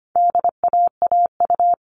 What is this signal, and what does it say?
Morse code, heard as a keyed tone, about 25 wpm, 700 Hz. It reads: DAAU
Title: All Cut Numbers - 4-Digit 25wpm